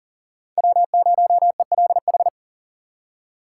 Morse code, heard as a keyed tone, about 40 wpm, 700 Hz. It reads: W0ELH
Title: USA Call Signs - 2x Character Spacing - 40wpm